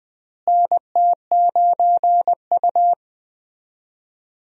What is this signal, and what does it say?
Morse code, heard as a keyed tone, about 20 wpm, 700 Hz. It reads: NT9U
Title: USA Call Signs - 4 Characters - Repeated 3x - 20wpm